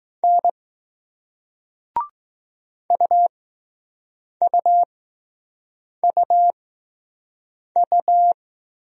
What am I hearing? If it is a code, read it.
Morse code, heard as a keyed tone, about 20 wpm, 700 Hz. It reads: N U U U U